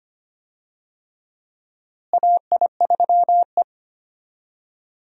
Morse code, heard as a keyed tone, about 25 wpm, 700 Hz. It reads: AI3E
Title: USA Call Signs - 4 Characters - Repeated 3x - 25wpm